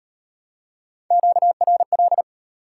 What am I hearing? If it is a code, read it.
Morse code, heard as a keyed tone, about 36 wpm, 700 Hz. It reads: QRL